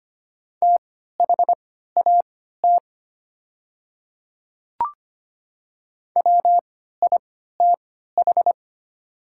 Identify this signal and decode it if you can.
Morse code, heard as a keyed tone, about 25 wpm, spaced as for 14 wpm, 700 Hz. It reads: THAT WITH
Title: 3x Character Spacing - Top 100 Words - 25wpm